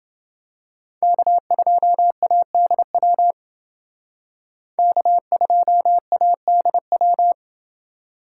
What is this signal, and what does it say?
Morse code, heard as a keyed tone, about 28 wpm, 700 Hz. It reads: K2ADW K2ADW